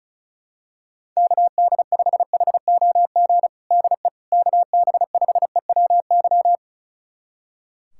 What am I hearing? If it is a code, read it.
Morse code, heard as a keyed tone, about 34 wpm, 700 Hz. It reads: KD5HOG DE KB5EWY
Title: USA Callsign DE Callsign - 6 Characters - 35wpm